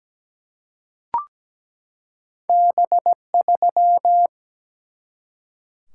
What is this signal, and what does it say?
Morse code, heard as a keyed tone, about 17 wpm, 700 Hz. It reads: B3
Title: Single-Letter Single-Number 17wpm